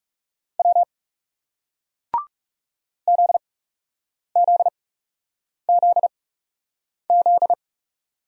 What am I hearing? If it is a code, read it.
Morse code, heard as a keyed tone, about 37 wpm, spaced as for 8 wpm, 700 Hz. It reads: W ZZZZ
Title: Speed-Racing Single Letters 30wpm